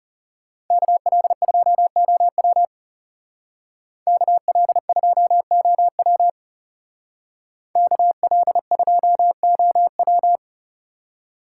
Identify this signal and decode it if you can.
Morse code, heard as a keyed tone, about 39 wpm, 700 Hz. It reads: KL2OW KL2OW KL2OW